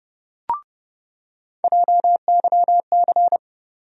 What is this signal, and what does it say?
Morse code, heard as a keyed tone, about 30 wpm, 700 Hz. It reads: JYC
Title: USA Call Sign Postfixes - 3 Letters 30wpm